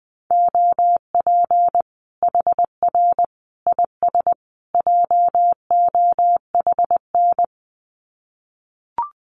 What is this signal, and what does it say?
Morse code, heard as a keyed tone, about 20 wpm, 700 Hz. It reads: OP HR IS JOHN
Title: Common QSO Phrases 20wpm